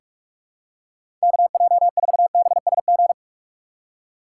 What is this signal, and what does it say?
Morse code, heard as a keyed tone, about 46 wpm, 700 Hz. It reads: KJ4BSG
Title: USA Call Signs - 6 Characters - 2x Character Spacing - 45wpm